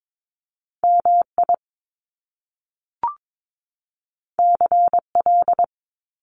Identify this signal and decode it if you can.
Morse code, heard as a keyed tone, about 22 wpm, 700 Hz. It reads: MI CL
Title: Top Two Letter Combinations in Words 22wpm